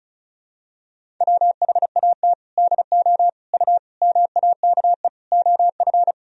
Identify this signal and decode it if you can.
Morse code, heard as a keyed tone, about 36 wpm, 700 Hz. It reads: WHAT DO U MAKE OF
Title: Sentences from Top 100 Words and Abbreviations 35wpm